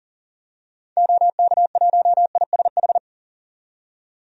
Morse code, heard as a keyed tone, about 40 wpm, 700 Hz. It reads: OK1ISH
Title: DXCC-100 Call Signs - 6 Characters - 40wpm